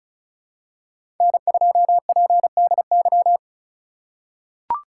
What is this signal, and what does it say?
Morse code, heard as a keyed tone, about 34 wpm, 700 Hz. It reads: N2PDY
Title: USA Call Signs - 5 Characters - 2x Character Spacing - 35wpm